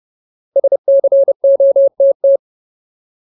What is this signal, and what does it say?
Morse code, heard as a keyed tone, about 30 wpm, 550 Hz. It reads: SCOTT
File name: Ultimate Word List / 30WPM Ultimate